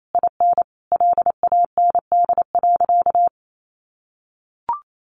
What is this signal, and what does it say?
Morse code, heard as a keyed tone, about 28 wpm, 700 Hz. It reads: IN LAND.
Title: Sentences from Top 1100 Words 28wpm